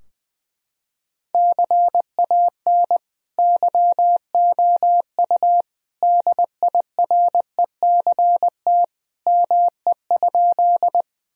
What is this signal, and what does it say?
Morse code, heard as a keyed tone, about 20 wpm, 700 Hz. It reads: CAN YOU DIRECT ME?
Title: Sentences from Top 1000 Words - Review Part 2 - 20wpm